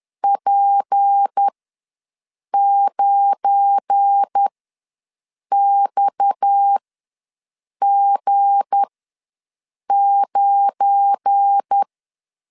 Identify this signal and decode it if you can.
Morse code, heard as a keyed tone, about 10 wpm, spaced as for 6 wpm, 800 Hz. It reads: P9XG9
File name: LEZIONI / 7 GF9Z / 7 GF9 ws 10wpm 4034